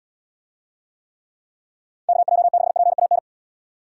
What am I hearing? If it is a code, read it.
Morse code, heard as a keyed tone, about 79 wpm, 700 Hz. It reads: KJ6PAN